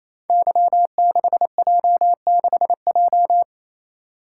Morse code, heard as a keyed tone, about 28 wpm, 700 Hz. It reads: Y6J6J
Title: Code Groups - 5 Letter:Numbers - 28wpm